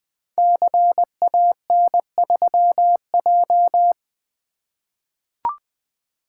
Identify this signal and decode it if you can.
Morse code, heard as a keyed tone, about 20 wpm, 700 Hz. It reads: CAN3J
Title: Code Groups - 5 Letter:Numbers - 20wpm